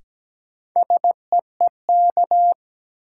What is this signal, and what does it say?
Morse code, heard as a keyed tone, about 17 wpm, 700 Hz. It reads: SEEK
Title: Sentences from Top 1600 Words - New Word Drill Alt - 17wpm